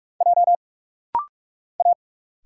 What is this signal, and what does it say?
Morse code, heard as a keyed tone, about 46 wpm, 700 Hz. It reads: J A
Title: Single Letters - Rapid-Fire 45wpm